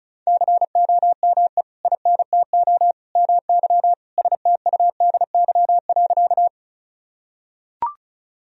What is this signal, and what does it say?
Morse code, heard as a keyed tone, about 34 wpm, 700 Hz. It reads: COME INTO MY STUDY.
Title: Sentences from Top 1000 Words - Review Part 1 - 35wpm